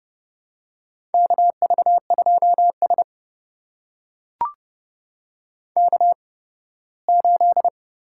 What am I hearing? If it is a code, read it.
Morse code, heard as a keyed tone, about 29 wpm, spaced as for 18 wpm, 700 Hz. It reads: KV2S K8
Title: USA Call Signs - 4 Characters - 8x Character Spacing - 30wpm